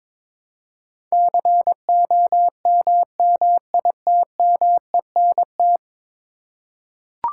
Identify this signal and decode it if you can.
Morse code, heard as a keyed tone, about 22 wpm, 700 Hz. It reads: COMMITMENT
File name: practice / 22wpm / 10 Letter Words - First 3 Letters Given - 22wpm